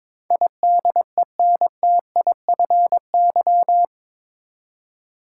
Morse code, heard as a keyed tone, about 21 wpm, 700 Hz. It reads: IDENTIFY
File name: practice / 22wpm / Eight Letter Words - 22wpm